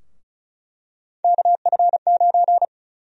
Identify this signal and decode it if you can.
Morse code, heard as a keyed tone, about 36 wpm, 700 Hz. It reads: KF9
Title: USA Call Sign Prefixes - Repeated 2x - 35wpm